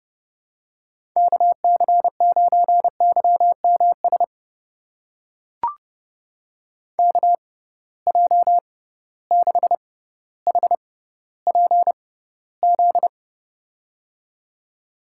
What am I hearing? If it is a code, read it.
Morse code, heard as a keyed tone, about 29 wpm, spaced as for 15 wpm, 700 Hz. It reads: KC9YMS KJ6HPZ
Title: USA Call Signs - 6 Characters - 6x Character Spacing - 30wpm